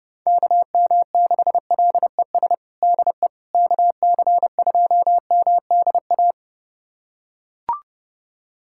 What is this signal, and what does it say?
Morse code, heard as a keyed tone, about 32 wpm, 700 Hz. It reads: KM6LES DE KC2MDA
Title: USA Callsign DE Callsign - 6 Characters - 30wpm